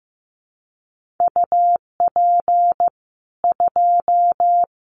Morse code, heard as a keyed tone, about 15 wpm, 700 Hz. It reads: UP 2
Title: Common QSO Phrases 15wpm